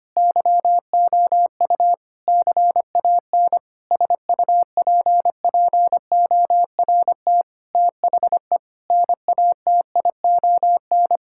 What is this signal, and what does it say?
Morse code, heard as a keyed tone, about 25 wpm, 700 Hz. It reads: YOU CAN SUPPORT THE NATION
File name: Sets of 9 Words - 25wpm